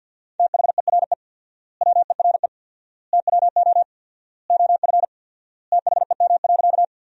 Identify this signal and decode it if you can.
Morse code, heard as a keyed tone, about 50 wpm, 700 Hz. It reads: THERE WERE TWO OF THEM.
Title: 4x Word-Spacing - Sentences from Top 100 Words - 50wpm